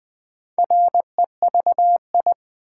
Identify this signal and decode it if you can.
Morse code, heard as a keyed tone, about 20 wpm, 700 Hz. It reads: REVI
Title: Build A Word - 2 Letters At A Time - 6-Letter Words - 20wpm